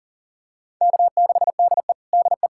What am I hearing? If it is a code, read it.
Morse code, heard as a keyed tone, about 40 wpm, 700 Hz. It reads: K6DE DE
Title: USA Callsign DE Callsign - 4 Characters - 40wpm